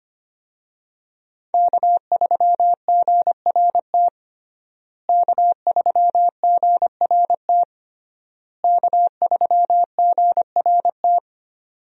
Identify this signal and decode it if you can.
Morse code, heard as a keyed tone, about 25 wpm, 700 Hz. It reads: K3GRT K3GRT K3GRT
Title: USA Call Signs - 5 Characters - Repeated 3x - 25wpm